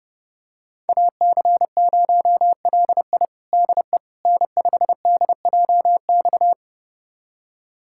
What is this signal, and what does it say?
Morse code, heard as a keyed tone, about 32 wpm, 700 Hz. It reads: AC0LI DE N5DJX
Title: USA Callsign DE Callsign 30wpm